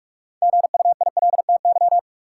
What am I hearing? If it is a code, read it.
Morse code, heard as a keyed tone, about 44 wpm, 700 Hz. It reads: GUILTY
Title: Six Letter Words - 45wpm